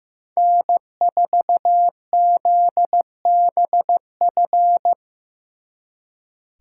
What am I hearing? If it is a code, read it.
Morse code, heard as a keyed tone, about 18 wpm, 700 Hz. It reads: N4ZBF
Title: Speed Racing - USA 1x3 Call Signs 15wpm